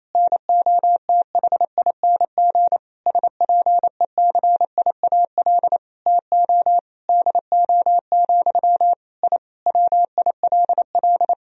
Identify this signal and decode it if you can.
Morse code, heard as a keyed tone, about 28 wpm, 700 Hz. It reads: NOTHING SPECIAL TO DO, I WILL
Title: Sentences from Top 1000 Words - Review Part 3 - 28wpm